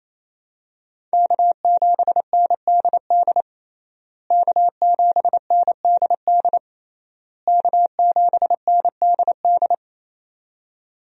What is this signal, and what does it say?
Morse code, heard as a keyed tone, about 28 wpm, 700 Hz. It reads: K7NDD K7NDD K7NDD
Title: USA Call Signs - 5 Characters - Repeated 3x - 28wpm